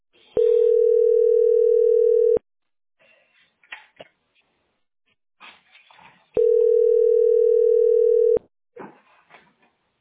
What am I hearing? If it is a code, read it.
Ringback tone (North America), 440 + 480 Hz.